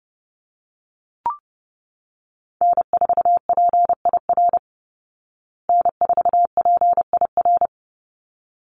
Morse code, heard as a keyed tone, about 30 wpm, 700 Hz. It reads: N4PIR N4PIR
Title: Call Signs Repeated 3x - 30wpm